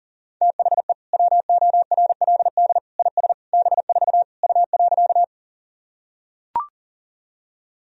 Morse code, heard as a keyed tone, about 40 wpm, 700 Hz. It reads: THE WORLD IS B4 U.
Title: Sentences from Top 200 Words and Abbreviations 40wpm